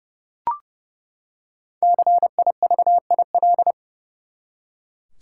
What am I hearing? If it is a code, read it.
Morse code, heard as a keyed tone, about 30 wpm, 700 Hz. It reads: CIVIL
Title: Five Letter Words - 30wpm